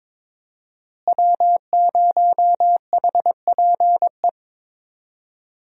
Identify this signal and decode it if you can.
Morse code, heard as a keyed tone, about 22 wpm, 700 Hz. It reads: W0HPE
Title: USA Call Signs - 5 Characters - Repeated 3x - 22wpm